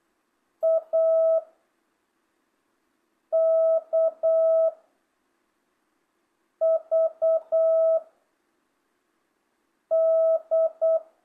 Morse code, heard as a keyed tone, about 8 wpm, 660 Hz. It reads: A K V D
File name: jh8cbh / 2025cw